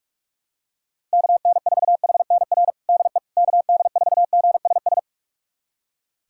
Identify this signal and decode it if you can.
Morse code, heard as a keyed tone, about 46 wpm, 690 Hz. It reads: KN4HNR DE KD4GSS
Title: USA Callsign DE Callsign - 6 Characters - 45wpm